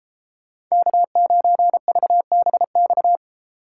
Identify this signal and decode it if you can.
Morse code, heard as a keyed tone, about 34 wpm, 700 Hz. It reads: K9VBX
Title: Speed Racing - USA 1x3 Call Signs 28wpm